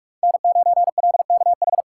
Morse code, heard as a keyed tone, about 46 wpm, 700 Hz. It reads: N9LKH
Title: USA Call Signs - 5 Characters - 4x Character Spacing - 45wpm